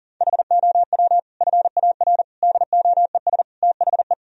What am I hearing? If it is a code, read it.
Morse code, heard as a keyed tone, about 40 wpm, 700 Hz. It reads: HOW FAR DOES THE
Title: Sentences from Top 1000 Words - Review Part 4 - 40wpm